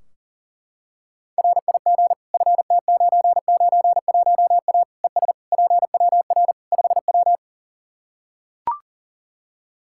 Morse code, heard as a keyed tone, about 37 wpm, 700 Hz. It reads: RIG FT991A ES PWR 5W
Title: Practice QSO - Exchange 2 - Info - 40wpm